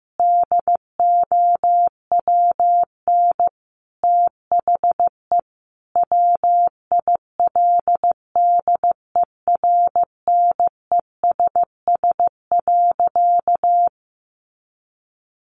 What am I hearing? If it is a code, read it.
Morse code, heard as a keyed tone, about 15 wpm, 700 Hz. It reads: DOWN THE WILDERNESS.
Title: A Sound Of Thunder 15wpm